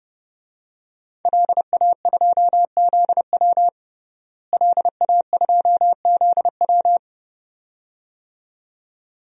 Morse code, heard as a keyed tone, about 30 wpm, 700 Hz. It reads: LA2ZW LA2ZW
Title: DXCC-100 Call Signs - 5 Characters - Repeated 2x - 30wpm